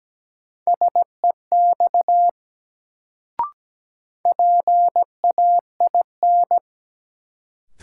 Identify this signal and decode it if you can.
Morse code, heard as a keyed tone, about 17 wpm, 700 Hz. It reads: SEX PAIN
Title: Sentences from Top 1600 Words - New Word Drill Alt - 17wpm